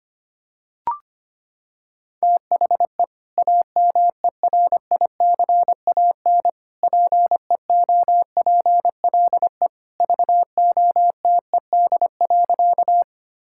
Morse code, heard as a keyed tone, about 25 wpm, 700 Hz. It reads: THE AMERICAN PEOPLE VOTED.
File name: practice / 25wpm / Sentences with 3-Word Phrases - 25wpm